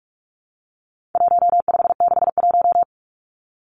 Morse code, heard as a keyed tone, about 44 wpm, 700 Hz. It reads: 1561
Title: Numbers - 4-Digit 45wpm